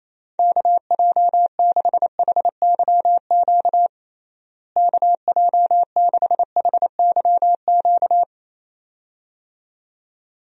Morse code, heard as a keyed tone, about 28 wpm, 700 Hz. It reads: KJ6HYQ KJ6HYQ
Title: USA Call Signs - 6 Characters - Repeated 2x - 28wpm